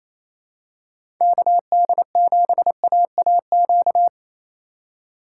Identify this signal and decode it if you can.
Morse code, heard as a keyed tone, about 28 wpm, 700 Hz. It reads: KD7AAQ
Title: USA Call Signs - 6 Characters - 2x Character Spacing - 28wpm